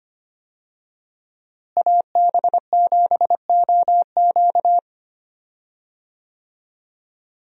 Morse code, heard as a keyed tone, about 25 wpm, 700 Hz. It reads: AB7OQ
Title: USA Call Signs - 5 Characters - Repeated 3x - 25wpm